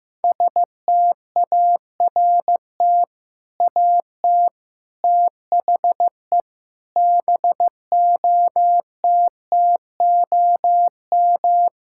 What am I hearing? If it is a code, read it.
Morse code, heard as a keyed tone, about 15 wpm, 700 Hz. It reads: START AT THE BOTTOM
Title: Sentences from Top 1000 Words - Review Part 6 - 15wpm